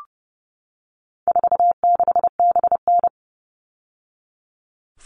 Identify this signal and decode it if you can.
Morse code, heard as a keyed tone, about 30 wpm, 700 Hz. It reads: 46BN
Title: All Cut Numbers - 4-Digit 30wpm